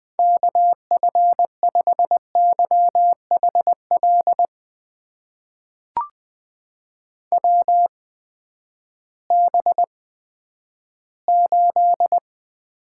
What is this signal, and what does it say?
Morse code, heard as a keyed tone, about 20 wpm, spaced as for 11 wpm, 700 Hz. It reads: KF5YHL WB8
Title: USA Call Signs - 6 Characters - 8x Character Spacing - 20wpm